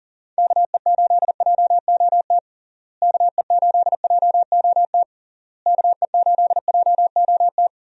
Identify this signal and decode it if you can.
Morse code, heard as a keyed tone, about 40 wpm, 700 Hz. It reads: KE8JOT KE8JOT KE8JOT
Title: USA Call Signs - 6 Characters - Repeated 3x - 40wpm